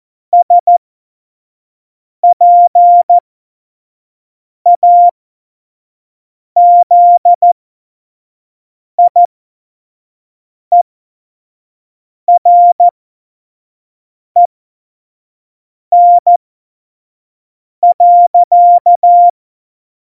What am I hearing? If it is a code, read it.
Morse code, heard as a keyed tone, about 14 wpm, spaced as for 5 wpm, 700 Hz. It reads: SPAZIEREN.
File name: Morsekurs / Witze / Witz 17.Juli 2018, 25 F70